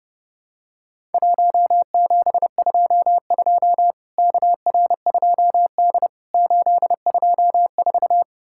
Morse code, heard as a keyed tone, about 30 wpm, 700 Hz. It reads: 1722 KR2D 824
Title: Worked All Europe DX Contest - QTC Exchange - 30wpm